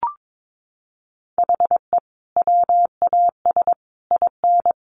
Morse code, heard as a keyed tone, about 22 wpm, 700 Hz. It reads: HE WAS IN
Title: Sentences from Top 1200 Words 22wpm